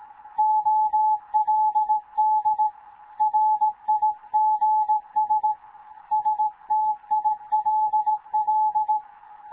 Morse code, heard as a keyed tone, about 18 wpm, 860 Hz. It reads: OLD RIGS STILL
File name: Morze-HA2MN / hangp / 2 11 GYAKORLAT